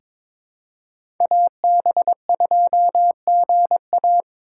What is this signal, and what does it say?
Morse code, heard as a keyed tone, about 22 wpm, 700 Hz. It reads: AB2GA
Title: USA Call Signs - 5 Characters - 2x Character Spacing - 22wpm